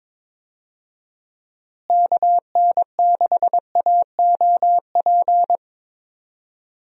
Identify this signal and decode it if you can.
Morse code, heard as a keyed tone, about 21 wpm, 700 Hz. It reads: KN6AOP
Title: USA Call Signs - 6 Characters - Repeated 2x - 22wpm